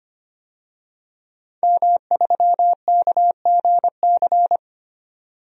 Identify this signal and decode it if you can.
Morse code, heard as a keyed tone, about 25 wpm, 700 Hz. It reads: M3KGC
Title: DXCC-100 Call Signs - 5 Characters - 25wpm